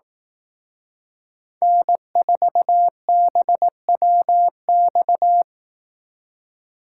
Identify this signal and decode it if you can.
Morse code, heard as a keyed tone, about 18 wpm, 700 Hz. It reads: N4BWX
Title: Speed Racing - USA Call Signs 15wpm